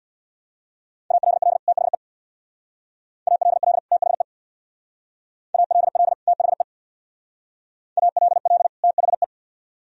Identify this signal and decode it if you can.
Morse code, heard as a keyed tone, about 60 wpm, 700 Hz. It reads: ALL THE ALL THE ALL THE ALL THE
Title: Speed-Racing - Sets of 2 Words 50wpm